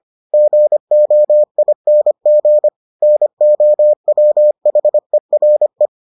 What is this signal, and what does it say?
Morse code, heard as a keyed tone, about 25 wpm, 600 Hz. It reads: GOING NOWHERE